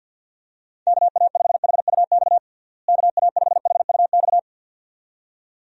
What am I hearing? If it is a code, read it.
Morse code, heard as a keyed tone, about 47 wpm, 700 Hz. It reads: KA5HUX KA5HUX
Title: USA Call Signs - 6 Characters - Repeated 2x - 50wpm